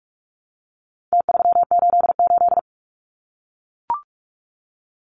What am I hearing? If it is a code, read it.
Morse code, heard as a keyed tone, about 44 wpm, 700 Hz. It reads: T388
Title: Numbers - Contest Style - 4-Digit 45wpm